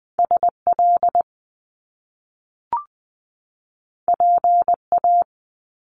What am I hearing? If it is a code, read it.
Morse code, heard as a keyed tone, about 20 wpm, 700 Hz. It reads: SL PA
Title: Top Two Letter Combinations in Words 20wpm